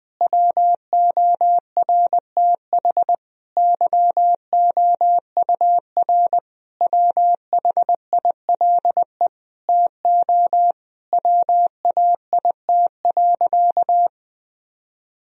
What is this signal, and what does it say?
Morse code, heard as a keyed tone, about 21 wpm, 700 Hz. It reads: WORTH YOUR WHILE TO WAIT.
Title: Sentences from Top 1000 Words - Review Part 1 - 20wpm